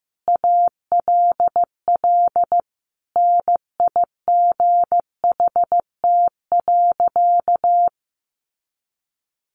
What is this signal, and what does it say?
Morse code, heard as a keyed tone, about 15 wpm, 700 Hz. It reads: ALL NIGHT.
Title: Sentences from Top 500 Words 15wpm